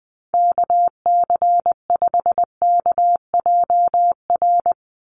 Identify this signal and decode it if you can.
Morse code, heard as a keyed tone, about 20 wpm, 700 Hz. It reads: KC5KJR
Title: USA Call Signs 20wpm